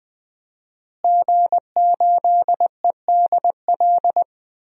Morse code, heard as a keyed tone, about 20 wpm, 700 Hz. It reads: G8EDL
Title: DXCC-100 Call Signs - 5 Characters - Repeated 2x - 20wpm